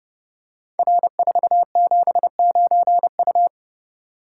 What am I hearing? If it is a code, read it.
Morse code, heard as a keyed tone, about 30 wpm, 700 Hz. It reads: R479U
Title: Code Groups - 5 Letter:Numbers - 30wpm